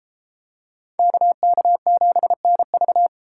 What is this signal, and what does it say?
Morse code, heard as a keyed tone, about 33 wpm, 700 Hz. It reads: KK7NV